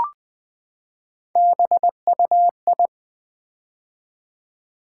Morse code, heard as a keyed tone, about 20 wpm, 700 Hz. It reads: BUI
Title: Build A Word - 3 Letters At A Time - 8-Letter Words - 20wpm